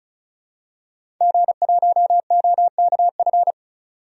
Morse code, heard as a keyed tone, about 34 wpm, 700 Hz. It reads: G1OKF
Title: DXCC-100 Call Signs - 5 Characters - 35wpm